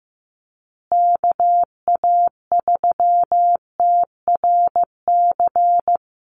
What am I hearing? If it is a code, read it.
Morse code, heard as a keyed tone, about 15 wpm, 700 Hz. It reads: KA3TRC
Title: USA Call Signs Repeated 3x 15wpm